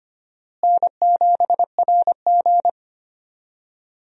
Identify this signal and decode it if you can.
Morse code, heard as a keyed tone, about 25 wpm, 700 Hz. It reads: N7RG
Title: USA Call Signs - 4 Characters - Repeated 2x - 25wpm